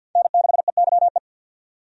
Morse code, heard as a keyed tone, about 51 wpm, 700 Hz. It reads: N6EYE
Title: USA Call Signs - 5 Characters - 50wpm